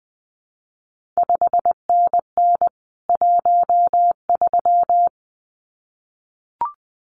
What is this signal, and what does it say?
Morse code, heard as a keyed tone, about 20 wpm, 700 Hz. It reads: 5NN 13
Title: CQ WW DX Contest Exchange - 20wpm